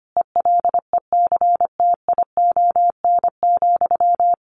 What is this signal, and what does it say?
Morse code, heard as a keyed tone, about 25 wpm, 700 Hz. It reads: ELECTION,
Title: Sentences from Top 1500 Words 25wpm